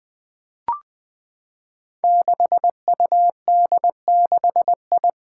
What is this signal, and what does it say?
Morse code, heard as a keyed tone, about 20 wpm, 700 Hz. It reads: 6UD6I
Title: Five Challenging Letter-Number Combinations 20wpm